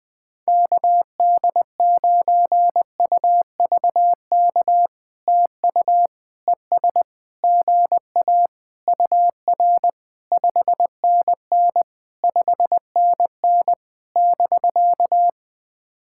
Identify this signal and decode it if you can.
Morse code, heard as a keyed tone, about 18 wpm, 700 Hz. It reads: KD9UVK TU ES GA UR 5NN 5NN <BK>
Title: POTA QSO Practice - 20wpm